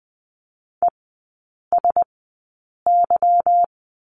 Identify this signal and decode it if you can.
Morse code, heard as a keyed tone, about 20 wpm, spaced as for 8 wpm, 700 Hz. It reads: ESY
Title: 152 Lesson 25e - Letter p - Three Characters 20wpm